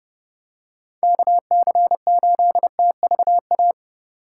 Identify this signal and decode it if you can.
Morse code, heard as a keyed tone, about 30 wpm, 700 Hz. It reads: KC8TVA